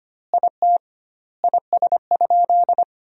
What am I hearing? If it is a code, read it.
Morse code, heard as a keyed tone, about 25 wpm, 700 Hz. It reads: IT IS?
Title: 2x Word-Spacing - Sentences from Top 100 Words - 25wpm